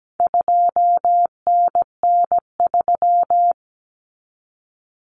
Morse code, heard as a keyed tone, about 17 wpm, 700 Hz. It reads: 2NN3
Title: Common Cut Numbers - 4-Digit 17wpm